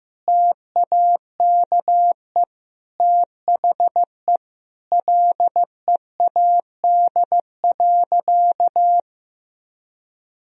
Morse code, heard as a keyed tone, about 15 wpm, 700 Hz. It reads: TAKE THE LEAD.
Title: Sentences from Top 1000 Words - Review Part 2 - 15wpm